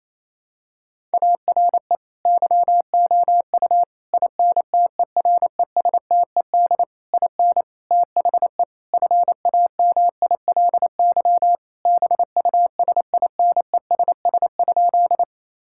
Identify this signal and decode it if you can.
Morse code, heard as a keyed tone, about 28 wpm, 700 Hz. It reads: ARE YOU INTERESTED IN THE FAMILY BUSINESS?
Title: Sentences from Top 1000 Words - Review Part 6 - 28wpm